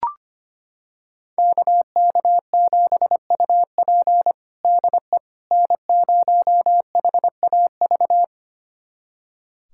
Morse code, heard as a keyed tone, about 25 wpm, 700 Hz. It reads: KK7UP DE N0HAV
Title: USA Callsign DE Callsign - 5 Characters - 25wpm